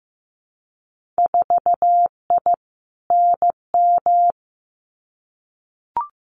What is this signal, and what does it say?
Morse code, heard as a keyed tone, about 15 wpm, 700 Hz. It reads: 4I NM
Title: Winter Field Day Exchanges Encore 15wpm